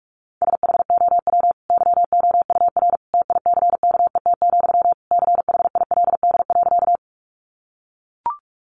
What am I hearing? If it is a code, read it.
Morse code, heard as a keyed tone, about 44 wpm, 700 Hz. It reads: SHOW YOUR TICKET, CHILD.
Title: Sentences from Top 1400 Words 45wpm